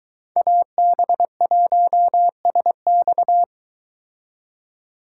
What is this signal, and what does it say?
Morse code, heard as a keyed tone, about 22 wpm, 700 Hz. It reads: AB1SX